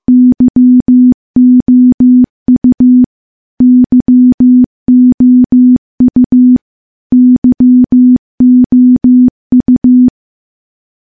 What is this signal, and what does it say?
Morse code, heard as a keyed tone, about 15 wpm, 250 Hz. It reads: YOU YOU YOU
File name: morse-training-15WPM